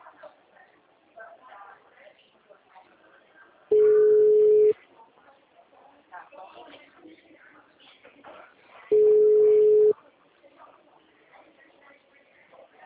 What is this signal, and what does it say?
Ringback tone (Europe, 425 Hz), with speech.
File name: out- -2014- - - .46893